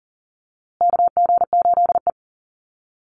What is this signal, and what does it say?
Morse code, heard as a keyed tone, about 37 wpm, 700 Hz. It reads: KG8E
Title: USA Call Signs Repeated 3x 40wpm